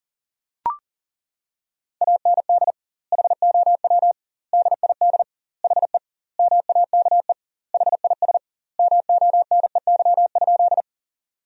Morse code, heard as a keyed tone, about 40 wpm, 700 Hz. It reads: AND HOW DID HE MAKE HIS MONEY?
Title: 2x Word-Spacing - Sentences from Top 400 Words - 40wpm